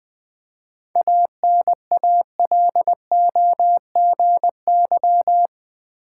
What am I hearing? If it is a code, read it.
Morse code, heard as a keyed tone, about 20 wpm, 690 Hz. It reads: ANALOGY
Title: 7 Letter Words - First 3 Letters Given - 20wpm